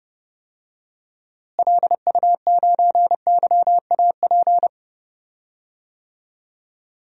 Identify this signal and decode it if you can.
Morse code, heard as a keyed tone, about 32 wpm, 700 Hz. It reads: LU9YAP
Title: DXCC-100 Call Signs - 6 Characters - Repeated 3x - 30wpm